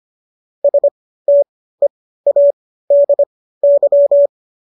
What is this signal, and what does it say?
Morse code, heard as a keyed tone, about 25 wpm, spaced as for 15 wpm, 570 Hz. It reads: STEADY